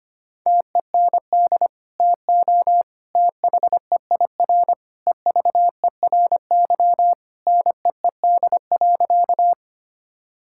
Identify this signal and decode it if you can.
Morse code, heard as a keyed tone, about 25 wpm, 700 Hz. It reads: TEND TO THEIR EVERY NEED.
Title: Sentences from Top 1000 Words - Review Part 5 - 25wpm